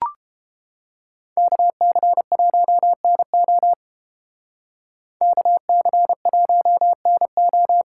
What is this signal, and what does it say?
Morse code, heard as a keyed tone, about 30 wpm, 700 Hz. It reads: KC1NO KC1NO